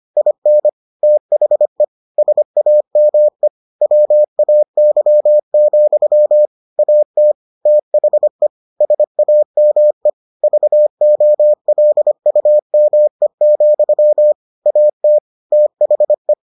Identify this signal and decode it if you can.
Morse code, heard as a keyed tone, about 25 wpm, 600 Hz. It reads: IN THE SAME WAY, AT THE SAME VOLUME, AT THE